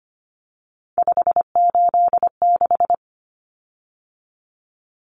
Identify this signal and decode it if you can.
Morse code, heard as a keyed tone, about 25 wpm, 700 Hz. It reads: 586
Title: Common Cut Numbers - 3-Digit 25wpm